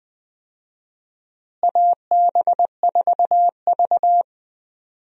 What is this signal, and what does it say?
Morse code, heard as a keyed tone, about 20 wpm, 700 Hz. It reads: AB4V
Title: USA Call Signs - 4 Characters - Repeated 2x - 20wpm